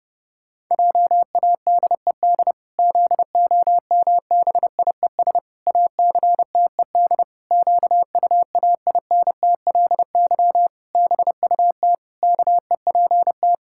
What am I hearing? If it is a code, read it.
Morse code, heard as a keyed tone, about 30 wpm, 700 Hz. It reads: JADED ZOMBIES ACTED QUAINTLY BUT KEPT
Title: Pangram Challenge v2 - 30wpm